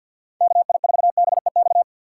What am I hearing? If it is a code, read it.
Morse code, heard as a keyed tone, about 50 wpm, 700 Hz. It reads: KI4BEX
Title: USA Call Signs - 6 Characters - 6x Character Spacing - 50wpm